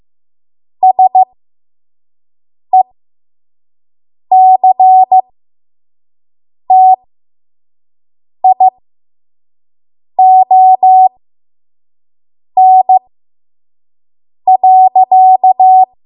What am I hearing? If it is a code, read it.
Morse code, heard as a keyed tone, about 15 wpm, spaced as for 5 wpm, 750 Hz. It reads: SECTION.
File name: Morse / 05WPM